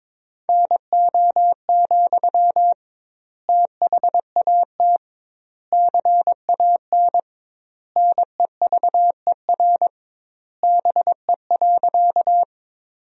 Morse code, heard as a keyed tone, about 22 wpm, 700 Hz. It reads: NO, THAT CAN NEVER BE.
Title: 2x Word-Spacing - Sentences from Top 300 Words - 22wpm